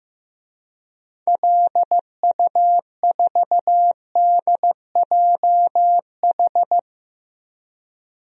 Morse code, heard as a keyed tone, about 15 wpm, 700 Hz. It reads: LU4DJH
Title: DXCC-100 Call Signs - 6 Characters - Repeated 2x - 15wpm